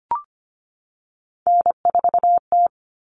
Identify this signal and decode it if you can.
Morse code, heard as a keyed tone, about 25 wpm, 700 Hz. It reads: N4T
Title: All Cut Numbers - 3-Digit 25wpm